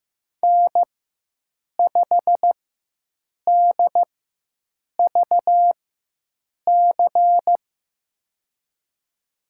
Morse code, heard as a keyed tone, about 15 wpm, spaced as for 7 wpm, 700 Hz. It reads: N5DVC
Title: USA Call Signs - 5 Characters - 4x Character Spacing - 15wpm